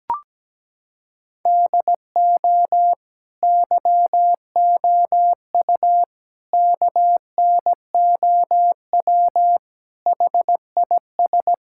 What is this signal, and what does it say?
Morse code, heard as a keyed tone, about 17 wpm, 700 Hz. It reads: DO YOU KNOW HIS
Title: Sentences from Top 1000 Words - Review Part 6 - 17wpm